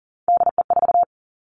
Morse code, heard as a keyed tone, about 41 wpm, 700 Hz. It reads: DE4
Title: Three Challenging Letter-Number Combinations 40wpm